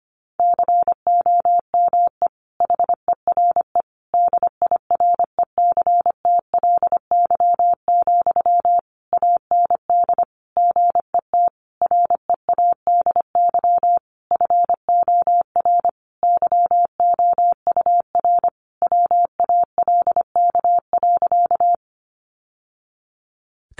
Morse code, heard as a keyed tone, about 25 wpm, 700 Hz. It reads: COME HERE DIRECTLY, AND GET READY FOR YOUR WALK.
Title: Sentences from Top 800 Words 25wpm